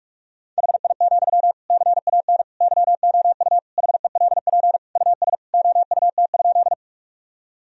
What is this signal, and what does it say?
Morse code, heard as a keyed tone, about 41 wpm, 700 Hz. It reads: HI, CAN YOU HELP US OUT?
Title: Sentences from Top 1000 Words - Review Part 3 - 45wpm